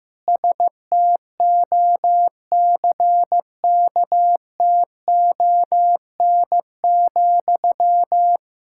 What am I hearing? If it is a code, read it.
Morse code, heard as a keyed tone, about 15 wpm, 700 Hz. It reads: STOCKTON,
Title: K1USN SST QSO Practice 15wpm